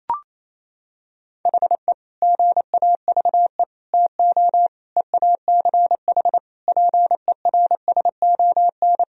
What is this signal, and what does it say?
Morse code, heard as a keyed tone, about 28 wpm, 700 Hz. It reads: HE GAVE TO EACH PERSON
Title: Sentences from Top 1000 Words - Review Part 4 - 28wpm